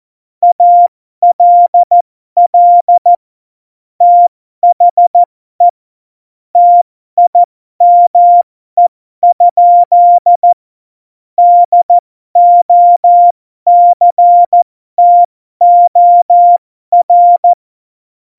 Morse code, heard as a keyed tone, about 14 wpm, spaced as for 12 wpm, 700 Hz. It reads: ALL THE TIME? DOCTOR
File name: Morsekurs / Witze